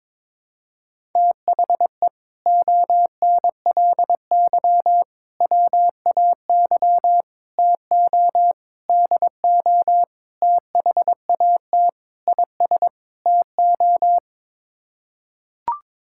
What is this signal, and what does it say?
Morse code, heard as a keyed tone, about 21 wpm, 700 Hz. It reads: THE ONLY WAY TO DO THAT IS TO
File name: practice / 22wpm / Sets of 8 Words - 22wpm